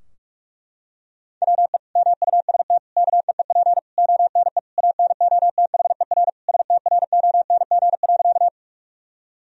Morse code, heard as a keyed tone, about 41 wpm, 700 Hz. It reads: WE MUST KEEP ONE ANOTHER STRONG.
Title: Sentences from Top 1000 Words - Review Part 3 - 45wpm